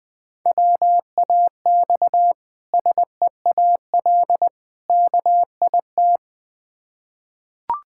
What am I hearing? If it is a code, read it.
Morse code, heard as a keyed tone, about 21 wpm, 700 Hz. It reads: WAX SEAL KIT
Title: Difficult Words with Hints - 20wpm